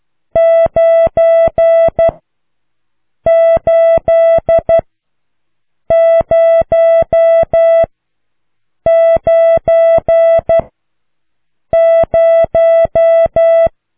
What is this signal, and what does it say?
Morse code, heard as a keyed tone, about 12 wpm, spaced as for 6 wpm, 660 Hz. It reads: 98090